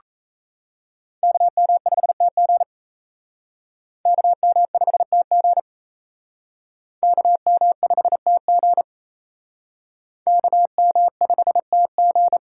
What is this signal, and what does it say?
Morse code, heard as a keyed tone, about 37 wpm, 700 Hz. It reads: KM5TG KM5TG KM5TG KM5TG